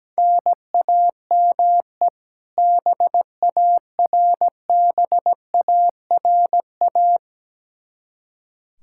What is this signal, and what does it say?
Morse code, heard as a keyed tone, about 17 wpm, 700 Hz. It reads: NAME BARBARA
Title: Practice QSO - Exchange 1 - Beginning - 17wpm